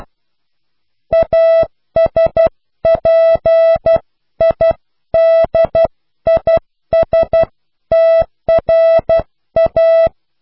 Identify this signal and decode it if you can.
Morse code, heard as a keyed tone, about 12 wpm, 660 Hz. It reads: ASPIDISTRA
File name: cw / pseudo worden(10)